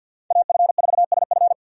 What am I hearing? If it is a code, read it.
Morse code, heard as a keyed tone, about 52 wpm, 700 Hz. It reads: AF4SF
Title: USA Call Signs - 5 Characters - 50wpm